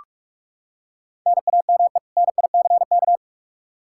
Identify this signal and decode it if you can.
Morse code, heard as a keyed tone, about 41 wpm, 700 Hz. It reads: NAME NICK
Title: Practice QSO - Exchange 1 - Beginning - 45wpm